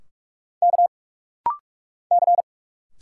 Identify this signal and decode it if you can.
Morse code, heard as a keyed tone, about 46 wpm, 700 Hz. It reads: K C
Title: Single Letters - Rapid-Fire 45wpm